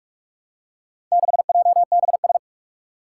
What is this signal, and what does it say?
Morse code, heard as a keyed tone, about 46 wpm, 700 Hz. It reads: 6JBS
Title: Four Challenging Letter-Number Combinations 45wpm